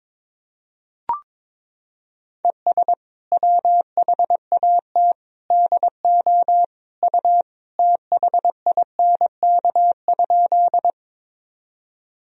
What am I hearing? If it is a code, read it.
Morse code, heard as a keyed tone, about 22 wpm, 700 Hz. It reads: ES WHAT DO U THINK?
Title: Sentences from Top 200 Words and Abbreviations 22wpm